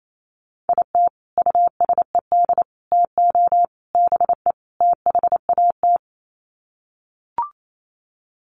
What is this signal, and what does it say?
Morse code, heard as a keyed tone, about 28 wpm, 700 Hz. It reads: IT USED TO BE THAT
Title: Sets of 5 Words Encore2 28wpm